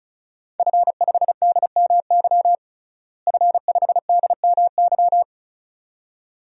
Morse code, heard as a keyed tone, about 36 wpm, 700 Hz. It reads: F5DMY F5DMY
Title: DXCC-100 Call Signs - 5 Characters - Repeated 2x - 35wpm